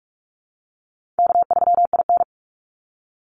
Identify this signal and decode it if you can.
Morse code, heard as a keyed tone, about 46 wpm, 700 Hz. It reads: K3IN
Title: Call Signs - 45wpm